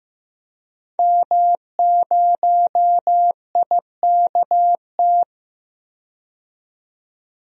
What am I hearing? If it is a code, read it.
Morse code, heard as a keyed tone, about 15 wpm, 700 Hz. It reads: M0IKT
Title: DXCC-100 Call Signs - 5 Characters - 15wpm